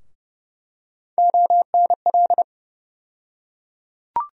Morse code, heard as a keyed tone, about 32 wpm, 700 Hz. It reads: ONL
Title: USA Call Sign Postfixes - 3 Letters 30wpm